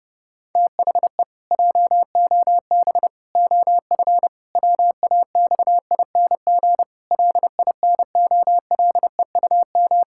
Morse code, heard as a keyed tone, about 30 wpm, 700 Hz. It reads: THE JOB OF WAXING LINOLEUM
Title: Pangram Challenge v2 - 30wpm